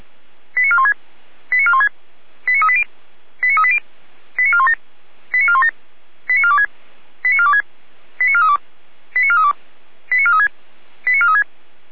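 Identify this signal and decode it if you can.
Five-tone selective call (ZVEI), heard as quick runs of five tones, tones of about 70 ms. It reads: 89416 89416 89380 89380 89417 89417 89426 89426 89432 89432 89436 89436
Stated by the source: Received on an SDR.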